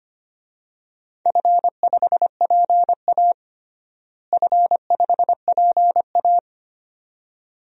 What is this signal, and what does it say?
Morse code, heard as a keyed tone, about 25 wpm, 700 Hz. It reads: F5PA F5PA
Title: DXCC-100 Call Signs - 4 Characters - Repeated 2x - 25wpm